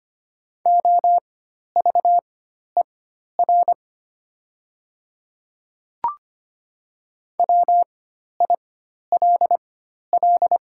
Morse code, heard as a keyed tone, about 25 wpm, spaced as for 12 wpm, 690 Hz. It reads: OVER WILL
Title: 4x Character Spacing - Top 300 Words - 25wpm